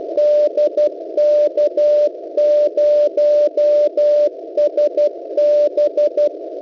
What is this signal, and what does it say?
Morse code, heard as a keyed tone, about 12 wpm, 600 Hz. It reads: DK0SB
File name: kiwisdr.local .3 cw